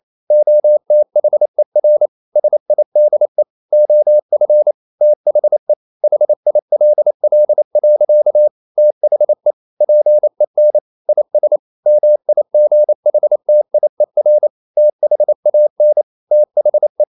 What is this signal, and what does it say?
Morse code, heard as a keyed tone, about 28 wpm, 600 Hz. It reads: OTHER SIDE OF THE HILL. THE PEN IS MIGHTIER THAN THE
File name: CW / Practice / EnglishSayings28